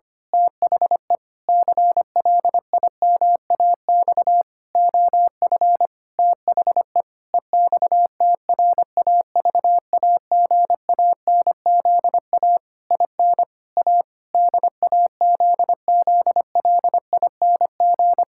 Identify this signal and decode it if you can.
Morse code, heard as a keyed tone, about 25 wpm, 700 Hz. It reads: THE CLIMAX OF THE EXTRAVAGANZA IN A DAZZLING
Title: Pangram Challenge v2 - 25wpm